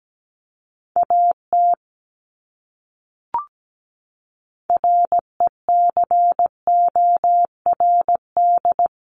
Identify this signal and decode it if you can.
Morse code, heard as a keyed tone, about 17 wpm, 700 Hz. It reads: AT RECORD
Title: Top 500 Words Repeated 17wpm